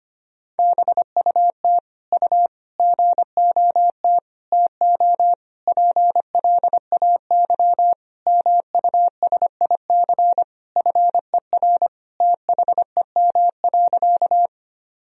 Morse code, heard as a keyed tone, about 25 wpm, 700 Hz. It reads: BUT U GOT TO PLAY MUSIC FER THEM.
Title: Sentences from Top 300 Words and Abbreviations 25wpm